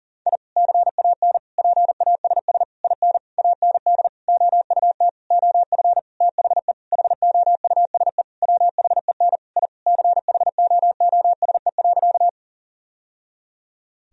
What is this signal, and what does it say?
Morse code, heard as a keyed tone, about 40 wpm, 700 Hz. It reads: I CAN PASS IN AND OUT OF THE HOUSE WHEN I CHOOSE.
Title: Sentences from Top 1000 Words - Review Part 4 - 40wpm